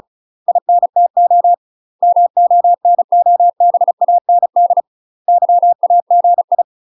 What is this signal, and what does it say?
Morse code, heard as a keyed tone, about 36 wpm, 700 Hz. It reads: INTO MONOBAND YAGI
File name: UK MASTER QSOs TRAINING / 35WPM / UK MASTER QSOs 0041